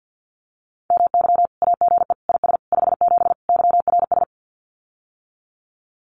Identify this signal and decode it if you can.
Morse code, heard as a keyed tone, about 50 wpm, 700 Hz. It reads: MY AGE IS 57 YRS
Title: Common QSO Phrases Encore 50wpm